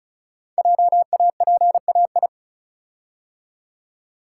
Morse code, heard as a keyed tone, about 36 wpm, 700 Hz. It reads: JAPAI
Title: Code Groups - 5 Letter:Numbers - 35wpm